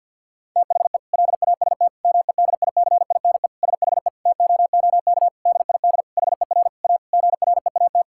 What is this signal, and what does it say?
Morse code, heard as a keyed tone, about 50 wpm, 700 Hz. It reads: THE LAST MEDICINE SHE TOOK DID HER A GREAT
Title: Sentences from Top 1000 Words - Review Part 6 - 50wpm